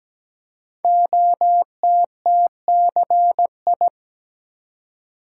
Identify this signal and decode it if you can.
Morse code, heard as a keyed tone, about 17 wpm, 700 Hz. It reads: OTTCI
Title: Code Groups - 5 Letters - 17wpm